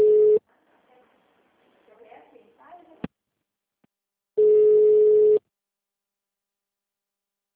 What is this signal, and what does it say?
Ringback tone (Europe, 425 Hz), with speech.